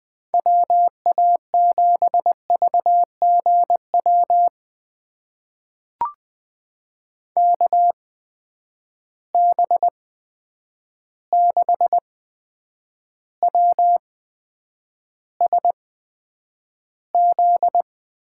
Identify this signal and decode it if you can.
Morse code, heard as a keyed tone, about 20 wpm, spaced as for 9 wpm, 700 Hz. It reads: WA7VGW KB6WSZ
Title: USA Call Signs - 8x Character Spacing - 20wpm